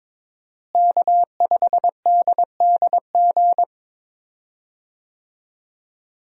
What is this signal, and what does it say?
Morse code, heard as a keyed tone, about 22 wpm, 700 Hz. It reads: K5DDG
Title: USA Call Signs - 5 Characters - Repeated 2x - 22wpm